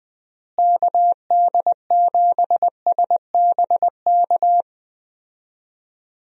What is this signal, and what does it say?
Morse code, heard as a keyed tone, about 20 wpm, 700 Hz. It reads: KD7SBK
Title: USA Call Signs - 6 Characters - Repeated 3x - 20wpm